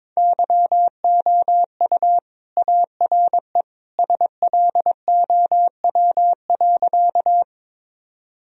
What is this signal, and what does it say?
Morse code, heard as a keyed tone, about 22 wpm, 700 Hz. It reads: YOU ARE SLOW.
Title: Sentences from Top 1000 Words - Review Part 2 - 22wpm